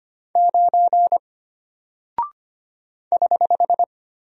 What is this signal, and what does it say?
Morse code, heard as a keyed tone, about 25 wpm, 700 Hz. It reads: 9 <HH>
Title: Single Character (Letter:Number:Punctuation:Prosign) Alt 25wpm